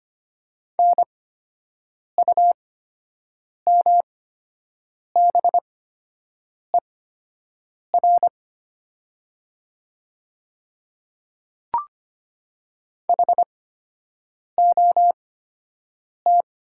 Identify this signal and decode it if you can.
Morse code, heard as a keyed tone, about 25 wpm, spaced as for 7 wpm, 690 Hz. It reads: NUMBER HOT
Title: 8x Character Spacing - Top 100 Words - 25wpm